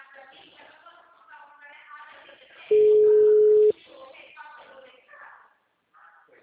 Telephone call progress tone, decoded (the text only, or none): ringback tone